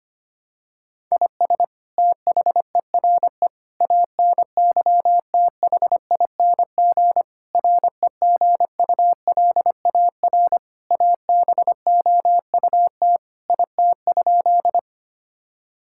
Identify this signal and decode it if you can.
Morse code, heard as a keyed tone, about 25 wpm, 700 Hz. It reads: IS THERE ANYTHING REGULAR ABOUT IT?
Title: Sentences from Top 1000 Words - Review Part 6 - 25wpm